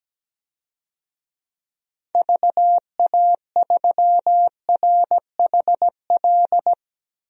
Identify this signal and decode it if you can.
Morse code, heard as a keyed tone, about 17 wpm, 700 Hz. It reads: VA3RHL
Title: DXCC-100 Call Signs - 6 Characters - 17wpm